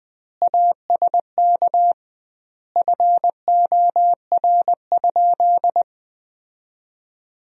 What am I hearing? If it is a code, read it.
Morse code, heard as a keyed tone, about 20 wpm, 700 Hz. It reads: ASK FOR?
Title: 2x Word-Spacing - Sentences from Top 200 Words - 20wpm